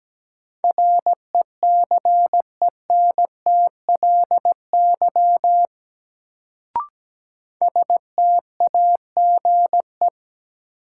Morse code, heard as a keyed tone, about 17 wpm, 700 Hz. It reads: RECENTLY STAGE
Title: Sentences from Top 1200 Words - New Word Drill Alt - 17wpm